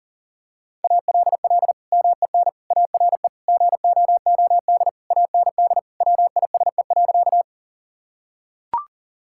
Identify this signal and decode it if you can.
Morse code, heard as a keyed tone, about 40 wpm, 700 Hz. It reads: ALL MEN ARE GOOD AND WISE.
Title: Sentences from Top 1600 Words 40wpm